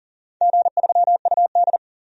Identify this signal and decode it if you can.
Morse code, heard as a keyed tone, about 39 wpm, 700 Hz. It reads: G3UD
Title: DXCC-100 Call Signs - 4 Characters - 40wpm